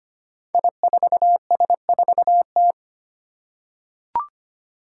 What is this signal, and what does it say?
Morse code, heard as a keyed tone, about 25 wpm, 700 Hz. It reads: I4S4T
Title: Five Challenging Letter-Number Combinations 25wpm